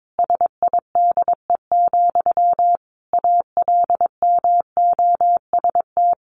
Morse code, heard as a keyed tone, about 22 wpm, 700 Hz. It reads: SIDE, ALMOST
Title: Sentences from Top 1300 Words 22wpm